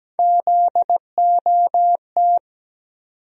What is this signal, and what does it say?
Morse code, heard as a keyed tone, about 17 wpm, 700 Hz. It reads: ZOT
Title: USA Call Sign Postfixes - 3 Letters 17wpm